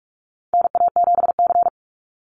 Morse code, heard as a keyed tone, about 46 wpm, 700 Hz. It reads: NA7C
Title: Call Signs - 45wpm